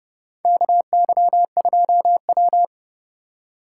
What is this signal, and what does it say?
Morse code, heard as a keyed tone, about 30 wpm, 700 Hz. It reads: KY2W
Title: USA Call Signs - 2x Character Spacing - 30wpm